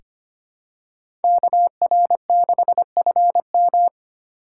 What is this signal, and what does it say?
Morse code, heard as a keyed tone, about 25 wpm, 700 Hz. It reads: KR6FM
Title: USA Call Signs - 8x Character Spacing - 25wpm